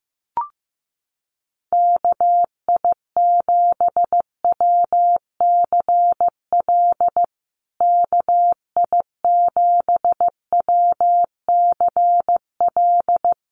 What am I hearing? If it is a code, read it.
Morse code, heard as a keyed tone, about 15 wpm, 700 Hz. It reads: KI7WCL KI7WCL
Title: USA Call Signs Repeated 3x 15wpm